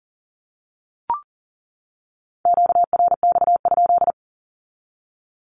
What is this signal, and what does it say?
Morse code, heard as a keyed tone, about 40 wpm, 700 Hz. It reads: QRX?
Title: Q-Codes 40wpm